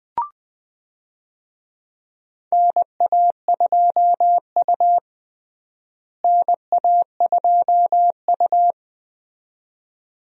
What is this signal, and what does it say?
Morse code, heard as a keyed tone, about 21 wpm, 700 Hz. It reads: NA2U NA2U
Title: USA Call Signs - 4 Characters - Repeated 2x - 20wpm